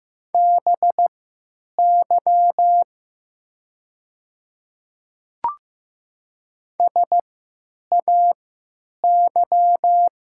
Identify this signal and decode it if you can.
Morse code, heard as a keyed tone, about 15 wpm, spaced as for 9 wpm, 690 Hz. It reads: BY SAY
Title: 3x Character Spacing - Top 300 Words - 15wpm